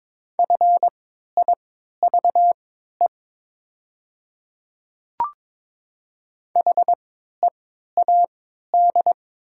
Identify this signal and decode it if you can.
Morse code, heard as a keyed tone, about 22 wpm, spaced as for 13 wpm, 700 Hz. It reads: FIVE HEAD
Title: 3x Character Spacing - Top 500 Words - 22wpm